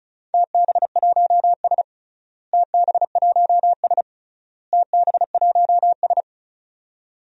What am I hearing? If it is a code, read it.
Morse code, heard as a keyed tone, about 36 wpm, 700 Hz. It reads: TB1S TB1S TB1S
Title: DXCC-100 Call Signs - 4 Characters - Repeated 3x - 35wpm